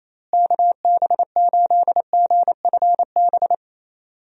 Morse code, heard as a keyed tone, about 28 wpm, 700 Hz. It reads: KB8GFB
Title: USA Call Signs - 6 Characters - Repeated 2x - 28wpm